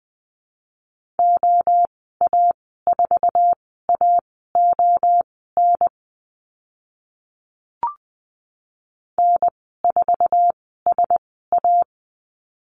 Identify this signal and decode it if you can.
Morse code, heard as a keyed tone, about 20 wpm, spaced as for 14 wpm, 700 Hz. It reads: OA4AON N4SA
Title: 048 Lesson 9g - Number 4 - Call Signs 20wpm